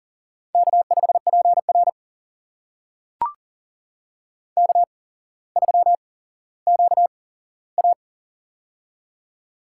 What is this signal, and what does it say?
Morse code, heard as a keyed tone, about 38 wpm, spaced as for 17 wpm, 700 Hz. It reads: K5PR K3QA
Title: USA Call Signs - 4 Characters - 8x Character Spacing - 40wpm